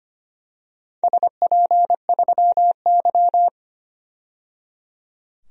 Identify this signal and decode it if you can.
Morse code, heard as a keyed tone, about 25 wpm, 700 Hz. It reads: SP3Y
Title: DXCC-100 Call Signs - 4 Characters - 25wpm